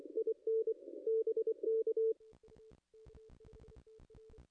Morse code, heard as a keyed tone, about 24 wpm, 440 Hz. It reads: IN BK
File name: contests / ss-cw-k5tr-op-n5rz -0200